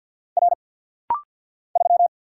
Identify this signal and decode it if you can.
Morse code, heard as a keyed tone, about 50 wpm, 690 Hz. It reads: R 3
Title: Single Letter-Number Alt - Rapid-Fire 50wpm